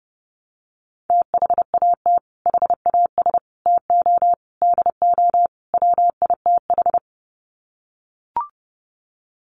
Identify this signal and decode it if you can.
Morse code, heard as a keyed tone, about 30 wpm, 700 Hz. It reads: THAT HAS TO DO WITH
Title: Sets of 5 Words Encore2 30wpm